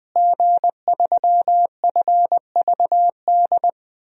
Morse code, heard as a keyed tone, about 20 wpm, 700 Hz. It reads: G3FVD
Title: DXCC-100 Call Signs - 5 Characters - Repeated 3x - 20wpm